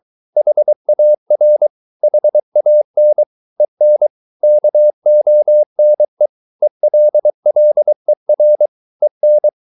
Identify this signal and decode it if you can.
Morse code, heard as a keyed tone, about 23 wpm, 600 Hz. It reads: HAR HAN EN KONE ELLER EN
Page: Ferie 23 wpm